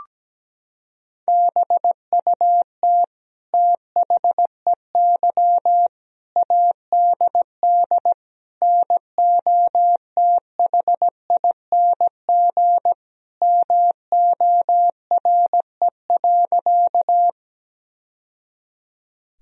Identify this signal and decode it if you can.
Morse code, heard as a keyed tone, about 17 wpm, 700 Hz. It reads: BUT THEY ADD NOTHING MORE.
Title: Sentences from Top 1000 Words - Review Part 1 - 17wpm